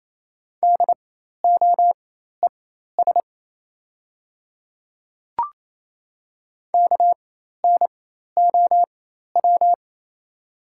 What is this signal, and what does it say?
Morse code, heard as a keyed tone, about 28 wpm, spaced as for 13 wpm, 690 Hz. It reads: DOES KNOW
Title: 4x Character Spacing - Top 200 Words - 28wpm